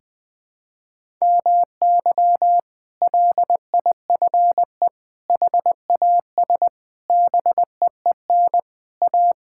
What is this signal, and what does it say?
Morse code, heard as a keyed tone, about 21 wpm, 700 Hz. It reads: MY LIFE HAS BEEN A
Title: Sentences from Top 1000 Words - Review Part 6 - 20wpm